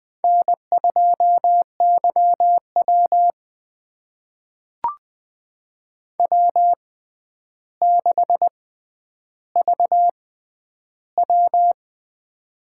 Morse code, heard as a keyed tone, about 20 wpm, spaced as for 10 wpm, 700 Hz. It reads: N2YW W6VW
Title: USA Call Signs - 4 Characters - 6x Character Spacing - 20wpm